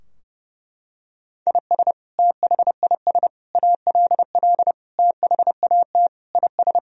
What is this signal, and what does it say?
Morse code, heard as a keyed tone, about 30 wpm, 700 Hz. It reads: IS THIS ALL THAT IS
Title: Sentences from Top 1000 Words - Review Part 3 - 30wpm